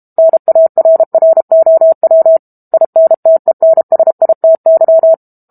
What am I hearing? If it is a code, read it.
Morse code, heard as a keyed tone, about 32 wpm, 650 Hz. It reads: NARROW INTENSITY
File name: Book Two / 06-Chapter 6 - 32 WPM